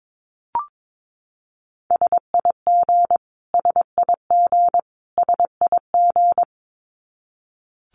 Morse code, heard as a keyed tone, about 22 wpm, 700 Hz. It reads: SIG SIG SIG
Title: Abbreviations Repeated 3x - 22wpm